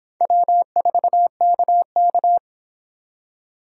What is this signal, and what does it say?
Morse code, heard as a keyed tone, about 26 wpm, 700 Hz. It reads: W4KK